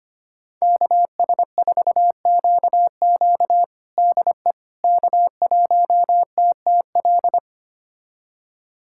Morse code, heard as a keyed tone, about 25 wpm, 700 Hz. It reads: KS4QQ DE K1TTL
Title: USA Callsign DE Callsign - 5 Characters - 25wpm